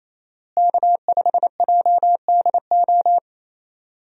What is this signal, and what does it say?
Morse code, heard as a keyed tone, about 28 wpm, 700 Hz. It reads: K5JDO
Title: USA Call Signs - 5 Characters - Repeated 3x - 28wpm